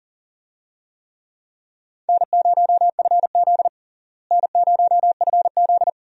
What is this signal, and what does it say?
Morse code, heard as a keyed tone, about 37 wpm, 700 Hz. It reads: N0FZ N0FZ
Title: USA Call Signs - 4 Characters - Repeated 2x - 40wpm